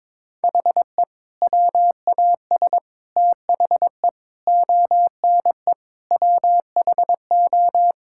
Morse code, heard as a keyed tone, about 22 wpm, 700 Hz. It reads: HE WAS THE ONE WHO
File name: practice / 22wpm / Sentences with 2-Word Phrases - 22wpm